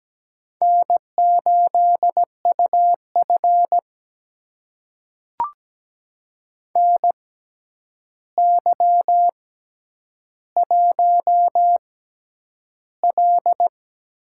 Morse code, heard as a keyed tone, about 17 wpm, spaced as for 9 wpm, 700 Hz. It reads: N8UF NY1L
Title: USA Call Signs - 4 Characters - 6x Character Spacing - 17wpm